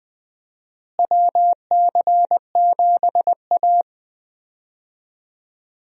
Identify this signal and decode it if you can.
Morse code, heard as a keyed tone, about 20 wpm, 700 Hz. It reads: WC7A
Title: USA Call Signs - 4 Characters - Repeated 2x - 20wpm